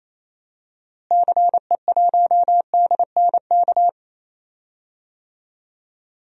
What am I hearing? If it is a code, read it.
Morse code, heard as a keyed tone, about 28 wpm, 700 Hz. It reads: CE1DNK
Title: DXCC-100 Call Signs - 6 Characters - Repeated 2x - 28wpm